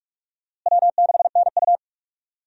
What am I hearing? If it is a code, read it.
Morse code, heard as a keyed tone, about 46 wpm, 700 Hz. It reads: W6NU
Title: USA Call Signs - 4 Characters - Repeated 3x - 45wpm